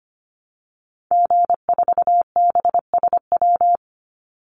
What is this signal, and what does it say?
Morse code, heard as a keyed tone, about 25 wpm, 700 Hz. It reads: G4BSW
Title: Call Signs Repeated 2x - 25wpm